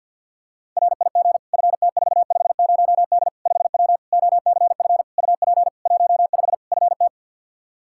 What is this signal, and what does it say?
Morse code, heard as a keyed tone, about 53 wpm, 700 Hz. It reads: RIG FT450D 5W OCF UP 15 FT
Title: K1USN SST QSO Practice 50wpm